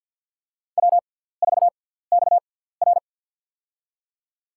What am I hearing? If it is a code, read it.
Morse code, heard as a keyed tone, about 53 wpm, spaced as for 17 wpm, 690 Hz. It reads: W4XR
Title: USA Call Signs - 4 Characters - 6x Character Spacing - 50wpm